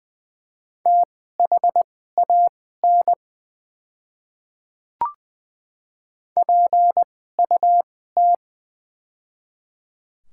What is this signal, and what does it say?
Morse code, heard as a keyed tone, about 20 wpm, spaced as for 15 wpm, 700 Hz. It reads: THAN PUT
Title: 2x Character Spacing - Top 300 Words - 20wpm